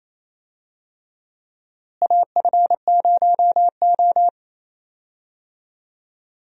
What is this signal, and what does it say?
Morse code, heard as a keyed tone, about 27 wpm, 700 Hz. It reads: AF0O
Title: USA Call Signs - 4 Characters - Repeated 3x - 28wpm